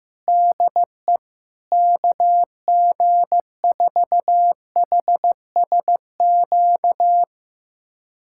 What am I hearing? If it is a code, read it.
Morse code, heard as a keyed tone, about 15 wpm, 700 Hz. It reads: DE KG4HSQ
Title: USA Callsign DE Callsign 15wpm